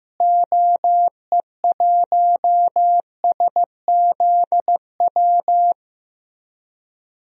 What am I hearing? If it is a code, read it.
Morse code, heard as a keyed tone, about 15 wpm, 700 Hz. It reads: OE1SZW
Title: DXCC-100 Call Signs - 6 Characters - Repeated 3x - 15wpm